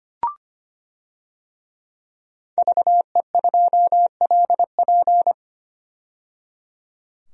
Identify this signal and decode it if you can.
Morse code, heard as a keyed tone, about 25 wpm, 700 Hz. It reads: VE2LP
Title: DXCC-100 Call Signs - 5 Characters - 25wpm